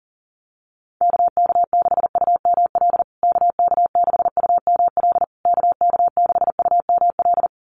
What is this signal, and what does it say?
Morse code, heard as a keyed tone, about 40 wpm, 700 Hz. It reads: KK6UML KK6UML KK6UML
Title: USA Call Signs Repeated 3x 40wpm